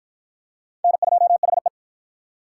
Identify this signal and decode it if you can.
Morse code, heard as a keyed tone, about 46 wpm, 700 Hz. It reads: N2HE